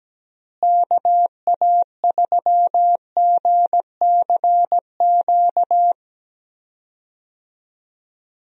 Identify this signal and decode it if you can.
Morse code, heard as a keyed tone, about 17 wpm, 700 Hz. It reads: KA3GCQ
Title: USA Call Signs - 6 Characters - Repeated 3x - 17wpm